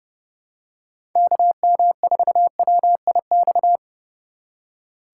Morse code, heard as a keyed tone, about 30 wpm, 700 Hz. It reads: KM4WIX
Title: USA Call Signs - 6 Characters - 2x Character Spacing - 30wpm